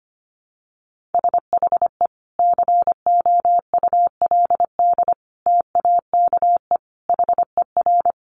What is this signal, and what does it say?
Morse code, heard as a keyed tone, about 25 wpm, 700 Hz. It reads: SHE COULD TAKE HER
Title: Sentences from Top 900 Words 25wpm